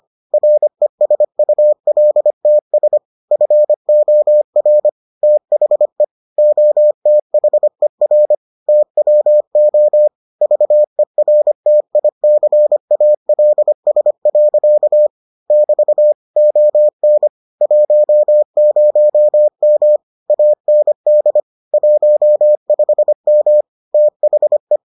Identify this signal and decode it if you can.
Morse code, heard as a keyed tone, about 25 wpm, 600 Hz. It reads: RESULTS FOR THE OTHER TWO VERTICALS. <BT> ON 10M AND 15M THE